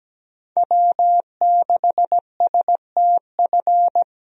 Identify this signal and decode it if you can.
Morse code, heard as a keyed tone, about 17 wpm, 700 Hz. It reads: W6STF
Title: USA Call Signs - 5 Characters - Repeated 2x - 17wpm